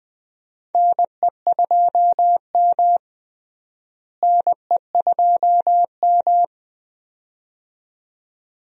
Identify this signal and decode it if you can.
Morse code, heard as a keyed tone, about 20 wpm, 700 Hz. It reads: NE2M NE2M
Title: USA Call Signs - 4 Characters - Repeated 2x - 20wpm